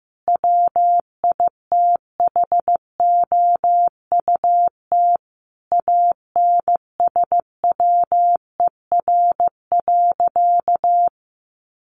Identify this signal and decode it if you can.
Morse code, heard as a keyed tone, about 15 wpm, 700 Hz. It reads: WITHOUT ANSWER.
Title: Sentences from Top 1500 Words 15wpm